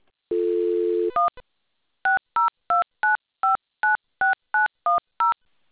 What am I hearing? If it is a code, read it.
DTMF, heard as telephone touch tones, tones of about 150 ms. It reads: 1 6029596910